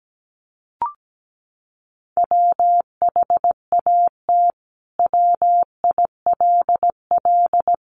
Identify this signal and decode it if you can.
Morse code, heard as a keyed tone, about 17 wpm, 700 Hz. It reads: WHAT WILL
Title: Sentences from Top 1300 Words 17wpm